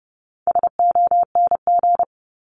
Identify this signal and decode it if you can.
Morse code, heard as a keyed tone, about 30 wpm, 700 Hz. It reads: SONG
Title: Top 500 Words Repeated 30wpm